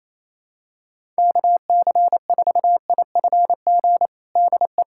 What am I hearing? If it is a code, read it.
Morse code, heard as a keyed tone, about 28 wpm, 700 Hz. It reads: KC4IFG DE
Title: USA Callsign DE Callsign - 6 Characters - 28wpm